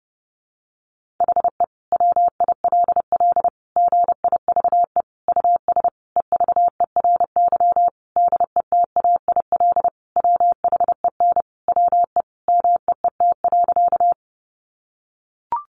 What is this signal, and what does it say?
Morse code, heard as a keyed tone, about 30 wpm, 700 Hz. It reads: HE WILL GIVE US EVERY DETAIL WHEN WE MEET.
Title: Sentences from Top 1500 Words 30wpm